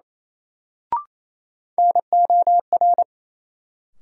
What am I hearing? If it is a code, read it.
Morse code, heard as a keyed tone, about 28 wpm, 700 Hz. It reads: NOR
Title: Sentences from Top 1500 Words - New Word Drill Alt - 28wpm